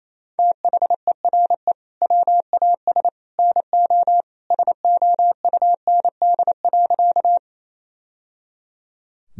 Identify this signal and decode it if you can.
Morse code, heard as a keyed tone, about 28 wpm, 700 Hz. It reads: THERE WAS NO SOUND.
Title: Sentences from Top 1000 Words - Review Part 1 - 28wpm